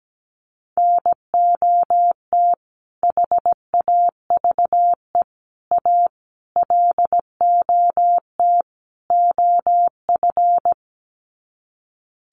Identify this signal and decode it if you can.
Morse code, heard as a keyed tone, about 17 wpm, 700 Hz. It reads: NOT HAVE A LOT OF
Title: Sets of 5 Words 17wpm